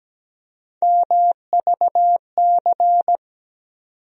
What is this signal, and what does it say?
Morse code, heard as a keyed tone, about 17 wpm, 700 Hz. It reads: MVC
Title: USA Call Sign Postfixes - 3 Letters 17wpm